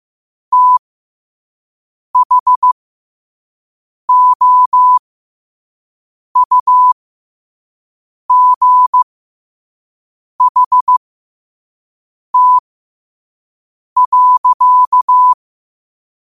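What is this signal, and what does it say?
Morse code, heard as a keyed tone, about 15 wpm, spaced as for 5 wpm, 1000 Hz. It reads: THOUGHT.
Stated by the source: Morse at 5 wpm.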